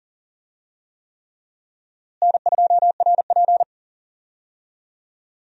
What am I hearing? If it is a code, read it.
Morse code, heard as a keyed tone, about 40 wpm, 700 Hz. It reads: N2RP
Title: USA Call Signs - 4 Characters - Repeated 3x - 40wpm